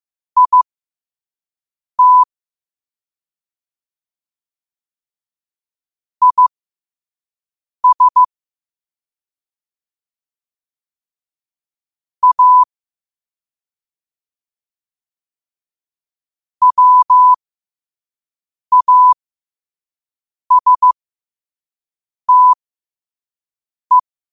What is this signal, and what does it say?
Morse code, heard as a keyed tone, about 15 wpm, spaced as for 5 wpm, 1000 Hz. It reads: IT IS A WASTE